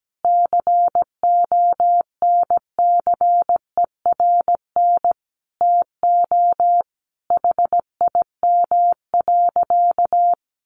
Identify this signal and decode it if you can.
Morse code, heard as a keyed tone, about 17 wpm, 700 Hz. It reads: CONCERN TO HIM.
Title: Sentences from Top 1200 Words 17wpm